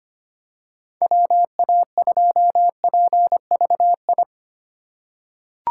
Morse code, heard as a keyed tone, about 25 wpm, 700 Hz. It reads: WA2PVI
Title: USA Call Signs - 6 Characters - 3x Character Spacing - 25wpm